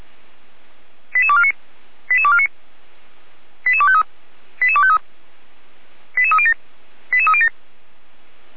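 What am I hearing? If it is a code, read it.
Five-tone selective call (ZVEI), heard as quick runs of five tones, tones of about 80 ms. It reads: 88249 88249 88253 88253 88307 88307
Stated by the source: ZVEI five-tone selective call received by SDR, logged by the Feuerwehr Orlamünde fire brigade.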